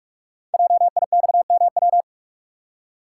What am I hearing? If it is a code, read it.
Morse code, heard as a keyed tone, about 44 wpm, 700 Hz. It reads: JIXMW
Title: Code Groups - 5 Letters - 45wpm